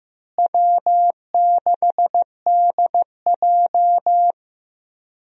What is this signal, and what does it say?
Morse code, heard as a keyed tone, about 15 wpm, 700 Hz. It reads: W6DJ
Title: USA Call Signs - 4 Characters - Repeated 3x - 15wpm